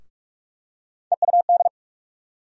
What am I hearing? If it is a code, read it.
Morse code, heard as a keyed tone, about 45 wpm, 700 Hz. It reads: EUD
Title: USA Call Sign Postfixes - 3 Letters 45wpm